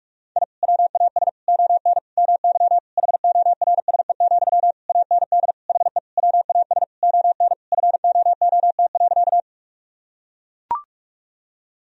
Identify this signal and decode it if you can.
Morse code, heard as a keyed tone, about 42 wpm, 700 Hz. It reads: I WAS ON MY HORSE, AND HE WAS ON FOOT.
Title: Sentences from Top 1000 Words - Review Part 3 - 45wpm